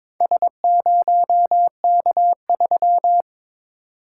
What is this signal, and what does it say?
Morse code, heard as a keyed tone, about 22 wpm, 690 Hz. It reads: S0K3
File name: practice / 22wpm / Four Letter-Number Combinations - 22wpm